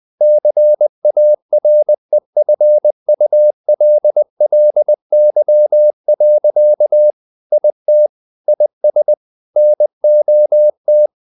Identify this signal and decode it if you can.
Morse code, heard as a keyed tone, about 20 wpm, 600 Hz. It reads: CAREFULLY. IT IS NOT
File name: the old man and the sea part 2 20wpm